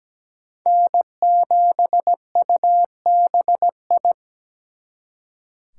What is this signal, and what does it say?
Morse code, heard as a keyed tone, about 20 wpm, 700 Hz. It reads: N7UBI
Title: Speed Racing - USA 1x3 Call Signs 17wpm